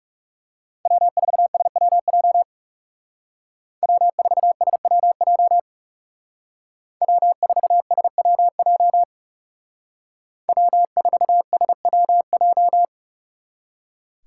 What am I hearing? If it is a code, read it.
Morse code, heard as a keyed tone, about 37 wpm, 700 Hz. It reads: W4SWJ W4SWJ W4SWJ W4SWJ
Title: Speed Racing - USA 1x3 Call Signs 30wpm